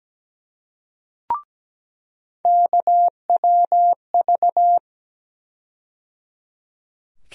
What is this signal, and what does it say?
Morse code, heard as a keyed tone, about 17 wpm, 700 Hz. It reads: KWV
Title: USA Call Sign Postfixes - 3 Letters 17wpm